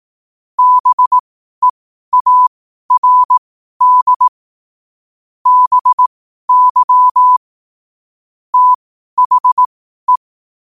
Morse code, heard as a keyed tone, about 18 wpm, spaced as for 12 wpm, 1000 Hz. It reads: BEARD BY THE
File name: qotdfiles / 12wpm